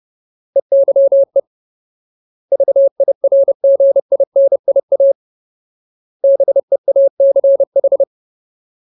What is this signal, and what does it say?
Morse code, heard as a keyed tone, about 30 wpm, 550 Hz. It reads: EYE VIRGINIA BEACH
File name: Ultimate Word List / 30WPM Ultimate